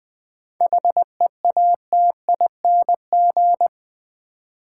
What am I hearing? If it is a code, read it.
Morse code, heard as a keyed tone, about 20 wpm, 700 Hz. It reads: HEATING
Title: 7 Letter Words - First 3 Letters Given - 20wpm